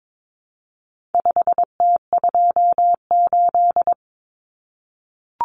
Morse code, heard as a keyed tone, about 22 wpm, 700 Hz. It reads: 5T28
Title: Common Cut Numbers - 4-Digit 22wpm